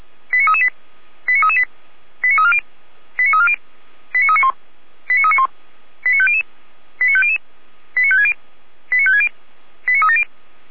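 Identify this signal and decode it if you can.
Five-tone selective call (ZVEI), heard as quick runs of five tones, tones of about 70 ms. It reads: 89338 89338 89340 89340 89391 89391 89500 89500 89560 89560 89370
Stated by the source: ZVEI five-tone call received with an SDR.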